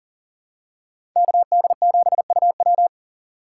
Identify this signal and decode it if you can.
Morse code, heard as a keyed tone, about 40 wpm, 700 Hz. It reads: KD7UW
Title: USA Call Signs - 5 Characters - 2x Character Spacing - 40wpm